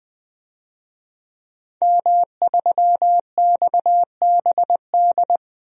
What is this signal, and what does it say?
Morse code, heard as a keyed tone, about 20 wpm, 700 Hz. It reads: M3XBD
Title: DXCC-100 Call Signs - 5 Characters - Repeated 3x - 20wpm